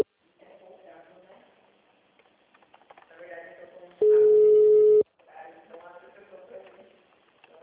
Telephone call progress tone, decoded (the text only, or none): ringback tone